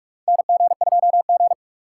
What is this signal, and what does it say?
Morse code, heard as a keyed tone, about 42 wpm, 700 Hz. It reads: NG2G
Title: USA Call Signs - 4 Characters - 3x Character Spacing - 45wpm